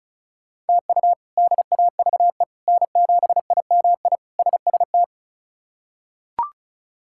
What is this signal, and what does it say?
Morse code, heard as a keyed tone, about 36 wpm, 700 Hz. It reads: TU DAVE N7IMI SST
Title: K1USN SST QSO Practice - 35wpm